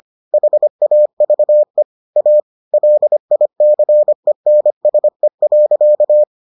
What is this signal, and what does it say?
Morse code, heard as a keyed tone, about 25 wpm, 600 Hz. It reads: HAVE A LICENSE.